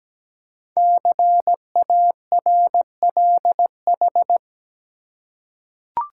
Morse code, heard as a keyed tone, about 17 wpm, 700 Hz. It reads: CARLH
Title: Code Groups - 5 Letters - 17wpm